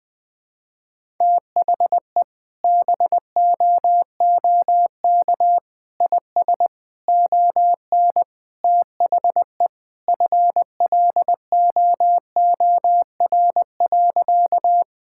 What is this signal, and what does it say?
Morse code, heard as a keyed tone, about 21 wpm, 700 Hz. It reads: THE BOOK IS ON THE FLOOR.
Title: Sentences with 3-Word Phrases - 20wpm